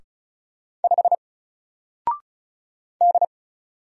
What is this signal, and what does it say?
Morse code, heard as a keyed tone, about 35 wpm, 700 Hz. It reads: 5 D
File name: Single-Letter-Number-35wpm